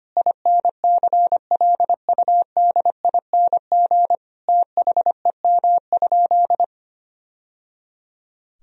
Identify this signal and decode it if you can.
Morse code, heard as a keyed tone, about 25 wpm, 700 Hz. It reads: INCLUDING THEM?
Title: Sentences from Top 1000 Words - Review Part 4 - 25wpm